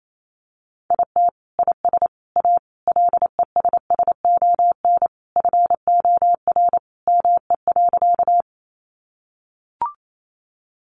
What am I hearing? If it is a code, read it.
Morse code, heard as a keyed tone, about 28 wpm, 700 Hz. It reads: IT IS A LESSON FOR ME.
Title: Sentences from Top 1200 Words 28wpm